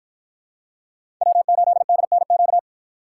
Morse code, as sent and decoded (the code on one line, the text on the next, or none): .-- ---.. -.. -. --.-
W8DNQ